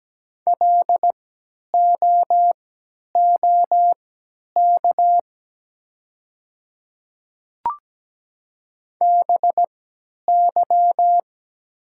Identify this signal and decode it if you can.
Morse code, heard as a keyed tone, about 17 wpm, spaced as for 10 wpm, 700 Hz. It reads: LOOK BY
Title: 3x Character Spacing - Top 300 Words - 17wpm